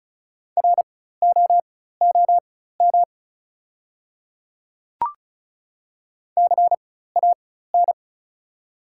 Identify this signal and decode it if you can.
Morse code, heard as a keyed tone, about 34 wpm, spaced as for 16 wpm, 690 Hz. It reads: ROOM CAN
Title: 4x Character Spacing - Top 300 Words - 35wpm